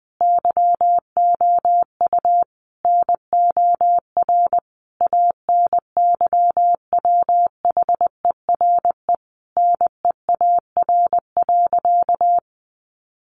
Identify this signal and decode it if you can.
Morse code, heard as a keyed tone, about 20 wpm, 700 Hz. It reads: YOU NOR ANYWHERE NEAR.
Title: Sentences from Top 1500 Words 20wpm